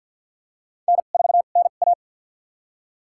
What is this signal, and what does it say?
Morse code, heard as a keyed tone, about 51 wpm, spaced as for 36 wpm, 700 Hz. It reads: N4NA
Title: USA Call Signs - 2x Character Spacing - 50wpm